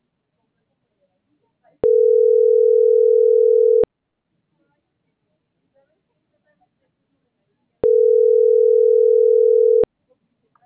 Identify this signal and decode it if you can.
Ringback tone (North America), 440 + 480 Hz.